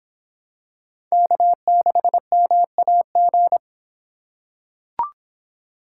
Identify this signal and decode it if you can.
Morse code, heard as a keyed tone, about 23 wpm, 700 Hz. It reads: K6MAG